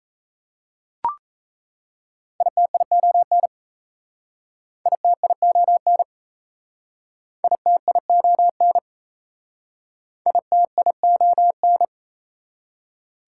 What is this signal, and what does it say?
Morse code, heard as a keyed tone, about 36 wpm, 690 Hz. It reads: ITION ITION ITION ITION